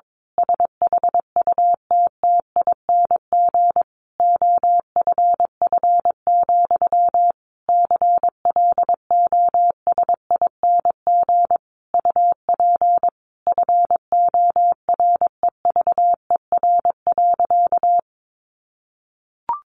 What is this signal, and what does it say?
Morse code, heard as a keyed tone, about 22 wpm, 700 Hz. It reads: SHUTTING OFF, CLOSING UP FOREVER.
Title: A Sound Of Thunder 22wpm